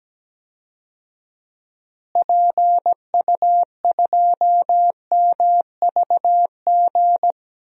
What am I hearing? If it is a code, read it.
Morse code, heard as a keyed tone, about 17 wpm, 700 Hz. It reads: PU2MVG
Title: DXCC-100 Call Signs - 6 Characters - Repeated 2x - 17wpm